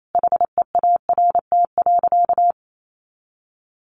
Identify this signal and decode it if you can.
Morse code, heard as a keyed tone, about 28 wpm, 700 Hz. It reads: HEART.
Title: Sentences from Top 1500 Words 28wpm